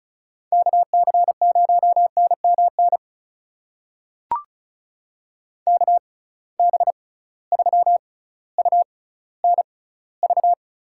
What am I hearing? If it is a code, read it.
Morse code, heard as a keyed tone, about 34 wpm, spaced as for 18 wpm, 700 Hz. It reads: KC0NMN KB3UNV
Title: USA Call Signs - 6 Characters - 6x Character Spacing - 35wpm